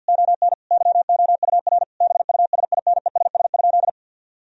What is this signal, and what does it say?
Morse code, heard as a keyed tone, about 50 wpm, 700 Hz. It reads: ON YOUR BUSINESS?
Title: Sentences from Top 1000 Words - Review Part 4 - 50wpm